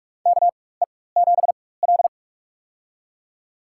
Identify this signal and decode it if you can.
Morse code, heard as a keyed tone, about 45 wpm, spaced as for 21 wpm, 690 Hz. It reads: KE7L
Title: USA Call Signs - 4x Character Spacing - 45wpm